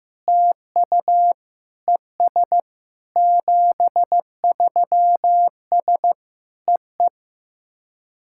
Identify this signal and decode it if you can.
Morse code, heard as a keyed tone, about 15 wpm, 700 Hz. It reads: TU ES 73S EE
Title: SOTA QSO Practice 15wpm